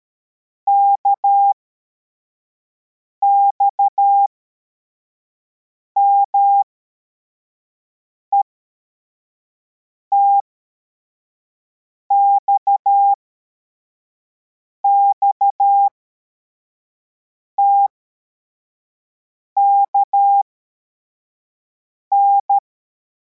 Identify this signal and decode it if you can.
Morse code, heard as a keyed tone, about 13 wpm, spaced as for 4 wpm, 800 Hz. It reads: KXMET XXTKN
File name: formation / cw / Text24